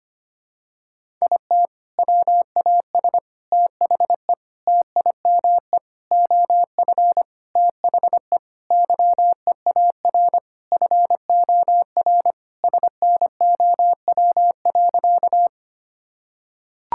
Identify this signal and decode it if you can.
Morse code, heard as a keyed tone, about 25 wpm, 700 Hz. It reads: IT WAS THE TIME OF THE YEAR FOR SNOW.
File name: practice / 25wpm / Sentences with 4-Word Phrases - 25wpm